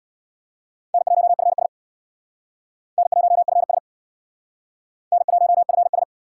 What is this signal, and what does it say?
Morse code, heard as a keyed tone, about 62 wpm, 700 Hz. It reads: N1FS N1FS N1FS
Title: Speed Racing - USA 1x2 Call Signs 50wpm